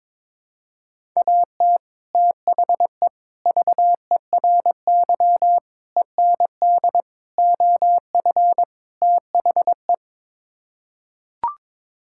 Morse code, heard as a keyed tone, about 22 wpm, 700 Hz. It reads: AT THE VERY END OF THE
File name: practice / 22wpm / Sets of 6 Words - 22wpm